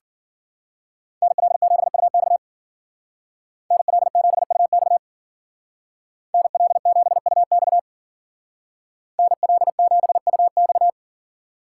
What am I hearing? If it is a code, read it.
Morse code, heard as a keyed tone, about 49 wpm, 700 Hz. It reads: NL7UX NL7UX NL7UX NL7UX
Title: Speed Racing - USA 2x2 Call Signs 40wpm